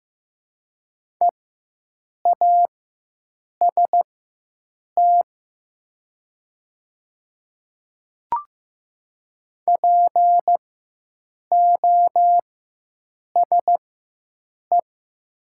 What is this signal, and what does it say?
Morse code, heard as a keyed tone, about 15 wpm, spaced as for 7 wpm, 700 Hz. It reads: EAST POSE
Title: 4x Character Spacing - Top 500 Words - 15wpm